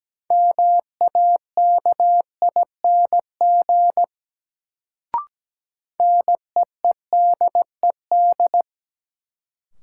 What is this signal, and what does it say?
Morse code, heard as a keyed tone, about 17 wpm, 700 Hz. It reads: MAKING NEEDED
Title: Sentences from Top 1000 Words - Review Part 6 - New Word Drill - 17wpm